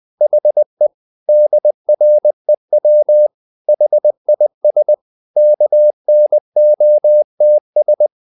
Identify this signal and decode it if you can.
Morse code, heard as a keyed tone, about 20 wpm, 600 Hz. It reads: HE DREW HIS KNOTS
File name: the old man and the sea / the old man and the sea part 1 20wpm